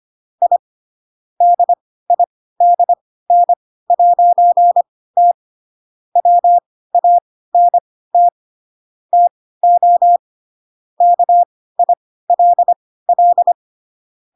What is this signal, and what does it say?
Morse code, heard as a keyed tone, about 25 wpm, spaced as for 16 wpm, 700 Hz. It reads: I DIDN'T WANT TO KILL
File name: Pinocchio / Pinocchio ch7 10 25 ik1hge